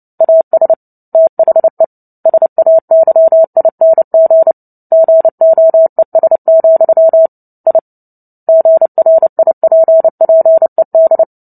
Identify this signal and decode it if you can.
Morse code, heard as a keyed tone, about 30 wpm, 650 Hz. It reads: AS THE SAYING GOES, I GRIPPED
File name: Book Two / 03-Chapter 3 - 29 WPM